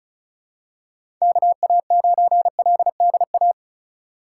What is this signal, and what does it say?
Morse code, heard as a keyed tone, about 36 wpm, 700 Hz. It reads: KA9LDA
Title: USA Call Signs - 6 Characters - 4x Character Spacing - 35wpm